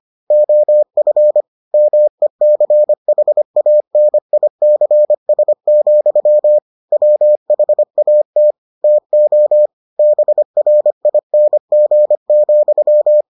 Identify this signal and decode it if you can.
Morse code, heard as a keyed tone, about 25 wpm, 600 Hz. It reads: OF MECHANICS, WHAT TO BRING,